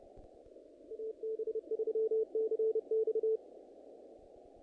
Morse code, heard as a keyed tone, about 30 wpm, 440 Hz. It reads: AB3CX
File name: contests / naqp-cw-k5tr-op-k5pi- -2000